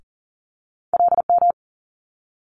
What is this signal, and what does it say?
Morse code, heard as a keyed tone, about 38 wpm, 700 Hz. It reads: LM
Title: Multi-letter Phonemes Quick Fire 40wpm